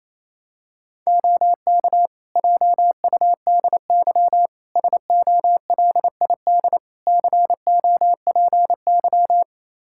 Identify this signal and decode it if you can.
Morse code, heard as a keyed tone, about 28 wpm, 700 Hz. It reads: OK JUDY SOLID COPY
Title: Practice QSO - Exchange 3 - More Info - 28wpm